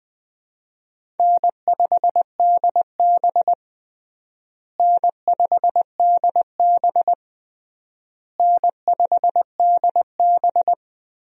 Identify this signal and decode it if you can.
Morse code, heard as a keyed tone, about 20 wpm, 700 Hz. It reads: N5DB N5DB N5DB
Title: USA Call Signs - 4 Characters - Repeated 3x - 20wpm